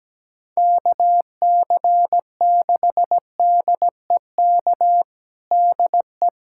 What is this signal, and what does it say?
Morse code, heard as a keyed tone, about 17 wpm, 700 Hz. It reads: KC6DEK DE
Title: USA Callsign DE Callsign - 6 Characters - 17wpm